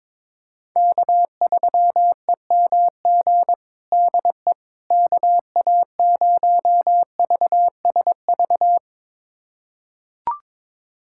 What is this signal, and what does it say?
Morse code, heard as a keyed tone, about 21 wpm, 700 Hz. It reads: K3EMG DE KA0VSV
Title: USA Callsign DE Callsign 22wpm